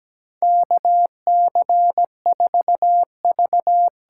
Morse code, heard as a keyed tone, about 17 wpm, 700 Hz. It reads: KC4V
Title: USA Call Signs - 6 Characters - 8x Character Spacing - 17wpm